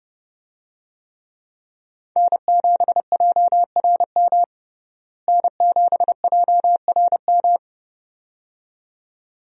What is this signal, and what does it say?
Morse code, heard as a keyed tone, about 30 wpm, 700 Hz. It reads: N7JRM N7JRM
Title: USA Call Signs - 5 Characters - Repeated 2x - 30wpm